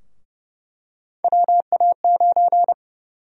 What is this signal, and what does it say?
Morse code, heard as a keyed tone, about 30 wpm, 700 Hz. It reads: WA9
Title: USA Call Sign Prefixes - 30wpm